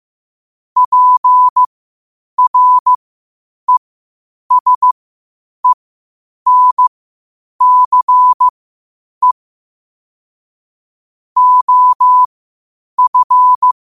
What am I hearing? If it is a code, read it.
Morse code, heard as a keyed tone, about 15 wpm, spaced as for 8 wpm, 1000 Hz. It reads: PRESENCE OF
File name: qotdfiles / 8wpm